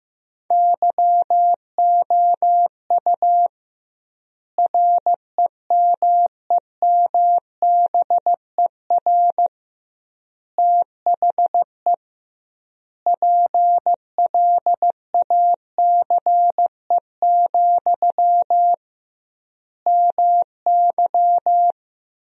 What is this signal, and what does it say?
Morse code, heard as a keyed tone, about 15 wpm, 700 Hz. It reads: YOU REMEMBER THE PLACE, MY
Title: 2x Word-Spacing - Sentences from Top 400 Words - 15wpm